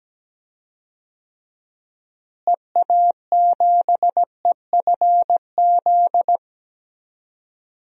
Morse code, heard as a keyed tone, about 17 wpm, 700 Hz. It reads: EA7EFZ
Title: DXCC-100 Call Signs - 6 Characters - Repeated 2x - 17wpm